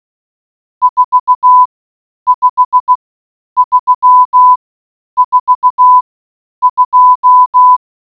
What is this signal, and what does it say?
Morse code, heard as a keyed tone, about 16 wpm, spaced as for 9 wpm, 1000 Hz. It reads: 45342